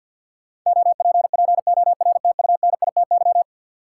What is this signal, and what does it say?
Morse code, heard as a keyed tone, about 50 wpm, 700 Hz. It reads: OPPORTUNITY
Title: Sentences from Top 1000 Words - New Word Drill - 50wpm